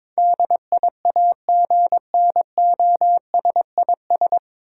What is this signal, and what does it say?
Morse code, heard as a keyed tone, about 22 wpm, 700 Hz. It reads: DIAGNOSIS
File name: 9 Letter Words - First 3 Letters Given - 22wpm